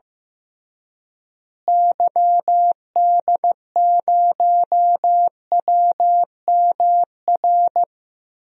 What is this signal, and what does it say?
Morse code, heard as a keyed tone, about 15 wpm, 700 Hz. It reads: YD0WMR
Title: DXCC-100 Call Signs - 6 Characters - Repeated 3x - 15wpm